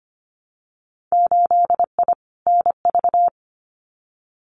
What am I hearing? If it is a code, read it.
Morse code, heard as a keyed tone, about 25 wpm, 700 Hz. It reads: 8I NV
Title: Winter Field Day Exchanges Encore 25wpm